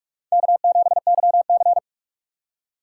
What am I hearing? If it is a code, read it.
Morse code, heard as a keyed tone, about 47 wpm, 700 Hz. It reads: K7YC
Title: USA Call Signs - 4 Characters - 3x Character Spacing - 45wpm